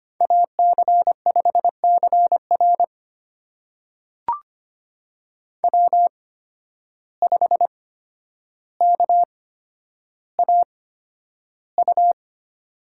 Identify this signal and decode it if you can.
Morse code, heard as a keyed tone, about 25 wpm, spaced as for 11 wpm, 700 Hz. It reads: AC5CR W5KAU
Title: USA Call Signs - 5 Characters - 8x Character Spacing - 25wpm